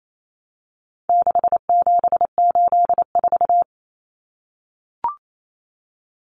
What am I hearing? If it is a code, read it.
Morse code, heard as a keyed tone, about 28 wpm, 700 Hz. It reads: 6784
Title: Common Cut Numbers - 4-Digit 28wpm